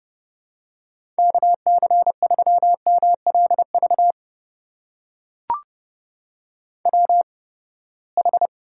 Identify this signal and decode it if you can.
Morse code, heard as a keyed tone, about 31 wpm, spaced as for 21 wpm, 700 Hz. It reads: KC3MLV WH
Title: USA Call Signs - 8x Character Spacing - 30wpm